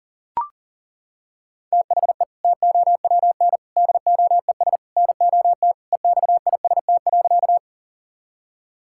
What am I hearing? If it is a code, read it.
Morse code, heard as a keyed tone, about 37 wpm, 700 Hz. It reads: THE TOWN DOES NOT EXIST.
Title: Sentences from Top 1000 Words - Review Part 5 - 40wpm